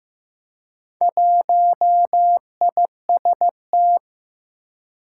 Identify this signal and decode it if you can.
Morse code, heard as a keyed tone, about 15 wpm, 700 Hz. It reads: 1IST
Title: Four Challenging Letter-Number Combinations 15wpm